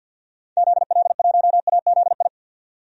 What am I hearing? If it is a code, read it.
Morse code, heard as a keyed tone, about 52 wpm, 700 Hz. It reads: ZL1AZI
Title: DXCC-100 Call Signs - 6 Characters - 50wpm